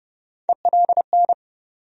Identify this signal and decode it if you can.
Morse code, heard as a keyed tone, about 30 wpm, 700 Hz. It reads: ELN
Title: USA Call Sign Postfixes - 3 Letters 30wpm